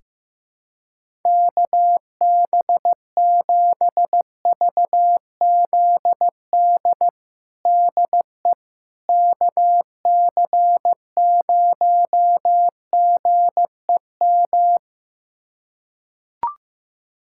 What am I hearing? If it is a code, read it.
Morse code, heard as a keyed tone, about 15 wpm, 700 Hz. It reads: KB7VZD DE KC0GEM
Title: USA Callsign DE Callsign 15wpm